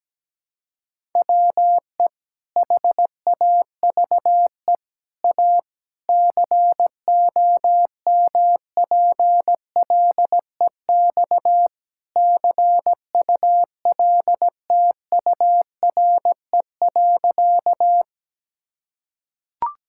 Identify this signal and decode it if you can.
Morse code, heard as a keyed tone, about 17 wpm, 700 Hz. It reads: WE HAVE A COMPLEX CULTURE.
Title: Sentences from Top 1000 Words - Review Part 6 - 17wpm